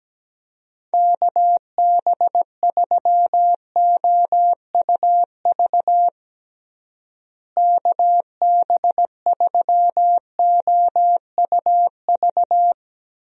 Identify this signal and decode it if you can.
Morse code, heard as a keyed tone, about 17 wpm, 700 Hz. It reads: KB3OUV KB3OUV
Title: USA Call Signs - 6 Characters - Repeated 2x - 17wpm